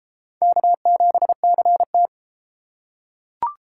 Morse code, heard as a keyed tone, about 29 wpm, 700 Hz. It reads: K7CT